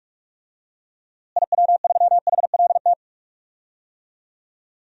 Morse code, heard as a keyed tone, about 42 wpm, 700 Hz. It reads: IW3HLT
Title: DXCC-100 Call Signs - 6 Characters - Repeated 3x - 45wpm